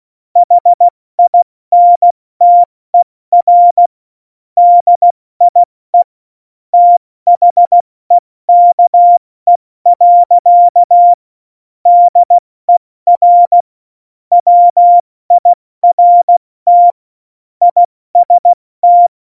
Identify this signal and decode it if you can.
Morse code, heard as a keyed tone, about 16 wpm, spaced as for 14 wpm, 700 Hz. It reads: HINTER DIE THEKE. DER WIRT IST
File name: Morsekurs / Witze / Witz 15. Oktober 2019, 70 F80